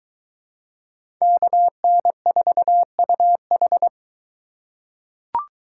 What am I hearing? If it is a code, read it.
Morse code, heard as a keyed tone, about 21 wpm, 700 Hz. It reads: KN4UH